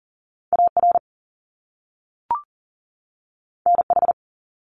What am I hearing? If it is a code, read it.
Morse code, heard as a keyed tone, about 38 wpm, 700 Hz. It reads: AR NH
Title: US States 40wpm